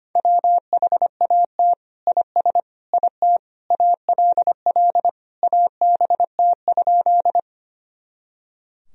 Morse code, heard as a keyed tone, about 25 wpm, 700 Hz. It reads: WHAT IS IT ALL ABT?
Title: Sentences from Top 100 Words and Abbreviations 25wpm